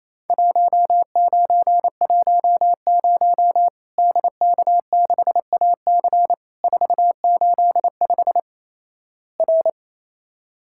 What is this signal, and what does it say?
Morse code, heard as a keyed tone, about 28 wpm, 700 Hz. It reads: 1910 DK6AC 485 R
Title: Worked All Europe DX Contest - QTC Exchange - 28wpm